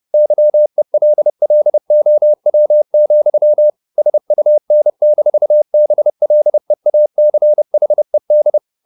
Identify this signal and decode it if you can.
Morse code, heard as a keyed tone, about 30 wpm, 600 Hz. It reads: YELLOW, SUN-BLEACHED
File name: the old man and the sea / the old man and the sea 1 30wpm